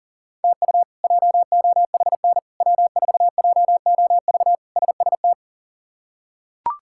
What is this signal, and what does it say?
Morse code, heard as a keyed tone, about 40 wpm, 700 Hz. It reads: TU JOHN W4JOV SST
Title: K1USN SST QSO Practice - 40wpm